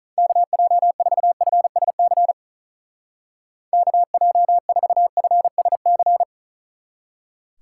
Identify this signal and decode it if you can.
Morse code, heard as a keyed tone, about 43 wpm, 700 Hz. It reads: KJ4FSC KJ4FSC
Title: Speed Racing - USA 2x3 Call Signs 35wpm